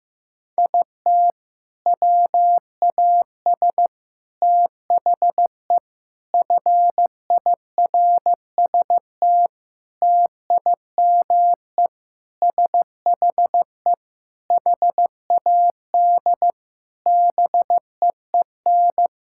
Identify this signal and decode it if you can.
Morse code, heard as a keyed tone, about 15 wpm, 700 Hz. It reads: IT WAS THE FIRST TIME SHE HAD BEEN
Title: Sentences from Top 1000 Words - Review Part 6 - 15wpm